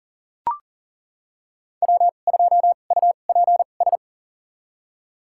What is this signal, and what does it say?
Morse code, heard as a keyed tone, about 40 wpm, spaced as for 29 wpm, 700 Hz. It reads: W2UPS
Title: USA Call Signs - 5 Characters - 2x Character Spacing - 40wpm